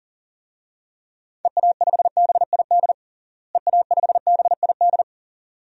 Morse code, heard as a keyed tone, about 40 wpm, 700 Hz. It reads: EA5BID EA5BID
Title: DXCC-100 Call Signs - 6 Characters - Repeated 2x - 40wpm